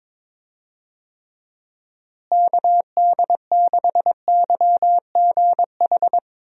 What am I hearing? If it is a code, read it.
Morse code, heard as a keyed tone, about 22 wpm, 700 Hz. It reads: KD6YGH
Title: USA Call Signs - 6 Characters - Repeated 2x - 22wpm